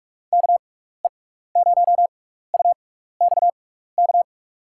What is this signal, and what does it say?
Morse code, heard as a keyed tone, about 46 wpm, spaced as for 16 wpm, 700 Hz. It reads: KE0UXK
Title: USA Call Signs - 6 Characters - 6x Character Spacing - 45wpm